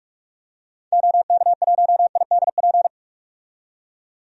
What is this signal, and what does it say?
Morse code, heard as a keyed tone, about 44 wpm, 700 Hz. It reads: OK1IDP
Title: DXCC-100 Call Signs - 6 Characters - 45wpm